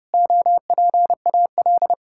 Morse code, heard as a keyed tone, about 30 wpm, 700 Hz. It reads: OPAL
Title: Pangram Challenge v2 - 30wpm